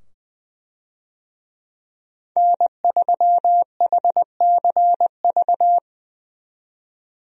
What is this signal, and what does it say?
Morse code, heard as a keyed tone, about 21 wpm, 700 Hz. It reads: N3HCV
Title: USA Call Signs - 5 Characters - 20wpm